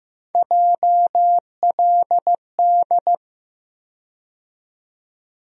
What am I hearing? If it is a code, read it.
Morse code, heard as a keyed tone, about 16 wpm, 700 Hz. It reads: JLD
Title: USA Call Sign Postfixes - 3 Letters 15wpm